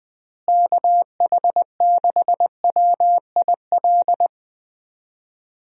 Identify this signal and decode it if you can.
Morse code, heard as a keyed tone, about 20 wpm, 700 Hz. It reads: KH6WIL
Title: USA Call Signs - 6 Characters - Repeated 3x - 20wpm